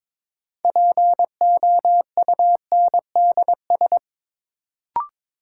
Morse code, heard as a keyed tone, about 22 wpm, 700 Hz. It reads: POUNDS
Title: Sentences from Top 1000 Words - Review Part 2 - New Word Drill - 22wpm